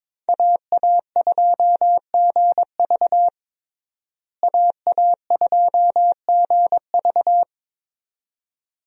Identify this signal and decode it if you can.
Morse code, heard as a keyed tone, about 22 wpm, 700 Hz. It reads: AA2GV AA2GV
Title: USA Call Signs - 5 Characters - Repeated 2x - 22wpm